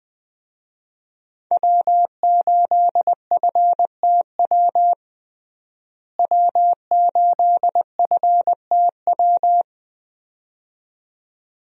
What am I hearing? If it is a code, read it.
Morse code, heard as a keyed tone, about 21 wpm, 700 Hz. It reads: W8FTW W8FTW
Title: USA Call Signs - 5 Characters - Repeated 2x - 20wpm